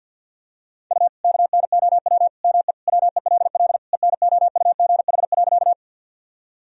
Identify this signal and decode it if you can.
Morse code, heard as a keyed tone, about 52 wpm, 700 Hz. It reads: U KNOW ME WELL ENOUGH.
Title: Sentences from Top 400 Words and Abbreviations 50wpm